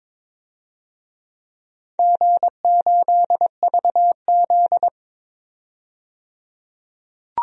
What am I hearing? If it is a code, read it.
Morse code, heard as a keyed tone, about 22 wpm, 700 Hz. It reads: G8VZ
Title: DXCC-100 Call Signs - 4 Characters - 22wpm